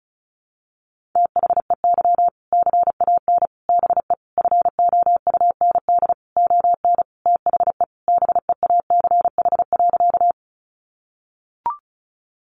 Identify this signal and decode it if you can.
Morse code, heard as a keyed tone, about 34 wpm, 700 Hz. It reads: THEY CAN BE FOUND ON THE BEACH.
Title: Sentences from Top 1500 Words 35wpm